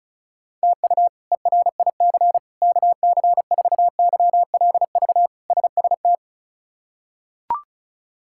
Morse code, heard as a keyed tone, about 34 wpm, 700 Hz. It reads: TU ERIC KC4YLV SST
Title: K1USN SST QSO Practice - 35wpm